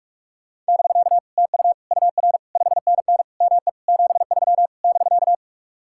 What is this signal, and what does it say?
Morse code, heard as a keyed tone, about 56 wpm, 700 Hz. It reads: <BK> TU UR 5NN ME 73 <BK>